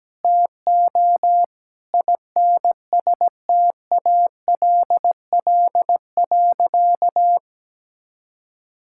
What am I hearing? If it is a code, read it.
Morse code, heard as a keyed tone, about 17 wpm, 700 Hz. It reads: TO INSTALL.
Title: Sentences from Top 1600 Words 17wpm